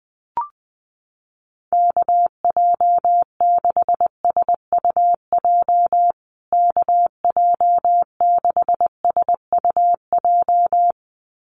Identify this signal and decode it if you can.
Morse code, heard as a keyed tone, about 20 wpm, 700 Hz. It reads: KJ6SUJ KJ6SUJ
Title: USA Call Signs Repeated 2x 20wpm